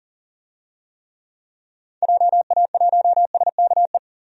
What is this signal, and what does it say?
Morse code, heard as a keyed tone, about 40 wpm, 700 Hz. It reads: JA1SKE
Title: DXCC-100 Call Signs - 6 Characters - 40wpm